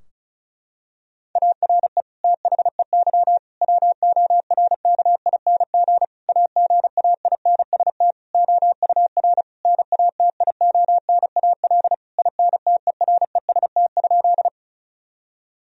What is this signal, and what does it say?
Morse code, heard as a keyed tone, about 36 wpm, 700 Hz. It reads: ARE THEY WORKING AGAINST OUR NATIONAL INTEREST?
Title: Sentences from Top 1000 Words - Review Part 4 - 35wpm